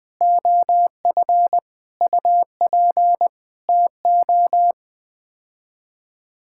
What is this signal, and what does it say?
Morse code, heard as a keyed tone, about 20 wpm, 700 Hz. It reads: OF UP TO
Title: Sets of 9 Words - 20wpm